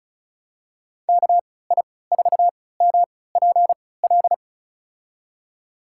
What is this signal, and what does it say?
Morse code, heard as a keyed tone, about 36 wpm, spaced as for 20 wpm, 700 Hz. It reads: KI4MPL
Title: USA Call Signs - 6 Characters - 3x Character Spacing - 35wpm